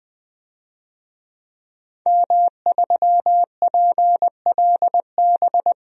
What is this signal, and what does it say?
Morse code, heard as a keyed tone, about 21 wpm, 700 Hz. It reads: M3PLB
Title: DXCC-100 Call Signs - 5 Characters - 20wpm